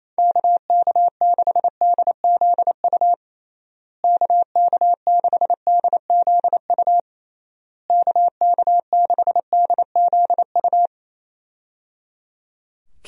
Morse code, heard as a keyed tone, about 28 wpm, 700 Hz. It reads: KK6DZU KK6DZU KK6DZU
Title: USA Call Signs - 6 Characters - Repeated 3x - 28wpm